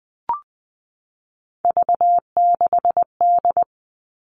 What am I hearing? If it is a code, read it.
Morse code, heard as a keyed tone, about 20 wpm, 700 Hz. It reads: V6D
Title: All Cut Numbers - 3-Digit 20wpm